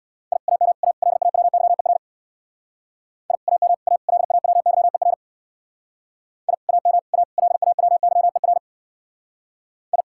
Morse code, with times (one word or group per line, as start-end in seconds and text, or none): 0.3-0.4 I
0.5-0.7 AM
0.8-0.9 A
1.0-2.0 LAWYER
3.3-3.4 I
3.5-3.7 AM
3.9-4.0 A
4.1-5.1 LAWYER
6.5-6.5 I
6.7-7.0 AM
7.1-7.2 A
7.4-8.6 LAWYER
9.9-10.0 I